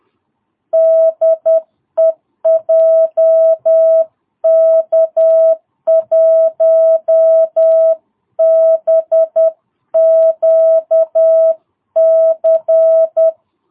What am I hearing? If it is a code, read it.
Morse code, heard as a keyed tone, about 10 wpm, 660 Hz. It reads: DE JK1BQC
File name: jh8cbh / 2022cw